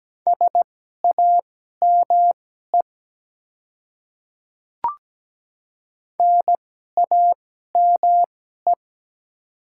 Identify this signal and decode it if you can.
Morse code, heard as a keyed tone, about 17 wpm, spaced as for 12 wpm, 700 Hz. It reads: SAME NAME
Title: 2x Character Spacing - Top 200 Words - 17wpm